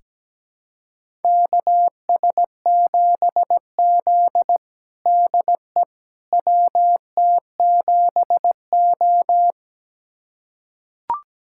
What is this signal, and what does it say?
Morse code, heard as a keyed tone, about 17 wpm, 700 Hz. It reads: KS7Z DE WT7O
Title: USA Callsign DE Callsign - 4 Characters - 17wpm